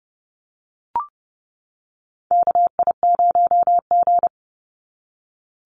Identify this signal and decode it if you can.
Morse code, heard as a keyed tone, about 30 wpm, 700 Hz. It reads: KI0G
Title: NA SOTA Chaser Callsigns 30wpm